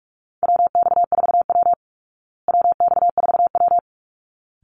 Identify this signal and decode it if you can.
Morse code, heard as a keyed tone, about 44 wpm, 700 Hz. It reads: WX4W WX4W
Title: Call Signs Repeated 3x - 45wpm